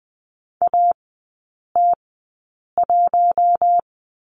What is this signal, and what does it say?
Morse code, heard as a keyed tone, about 20 wpm, spaced as for 8 wpm, 700 Hz. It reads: AT1
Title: 041 Lesson 8e - Number 1 - Three Characters 20wpm